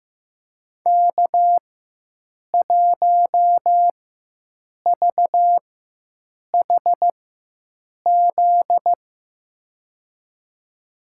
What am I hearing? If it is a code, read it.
Morse code, heard as a keyed tone, about 15 wpm, spaced as for 7 wpm, 700 Hz. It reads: K1VHZ
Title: USA Call Signs - 4x Character Spacing - 15wpm